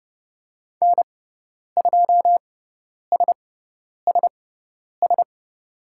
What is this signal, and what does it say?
Morse code, heard as a keyed tone, about 30 wpm, spaced as for 10 wpm, 700 Hz. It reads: N2SSS
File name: Lesson 17g - Letter u - Call Signs 30wpm